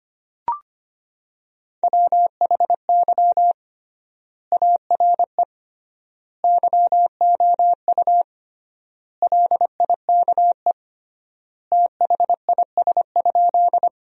Morse code, heard as a keyed tone, about 25 wpm, 700 Hz. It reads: WHY ARE YOU LIKE THIS?
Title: 3x Word-Spacing - Sentences from Top 200 Words - 25wpm